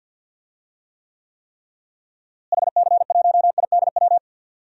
Morse code, heard as a keyed tone, about 52 wpm, 700 Hz. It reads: HC1IDW
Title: DXCC-100 Call Signs - 6 Characters - Repeated 3x - 50wpm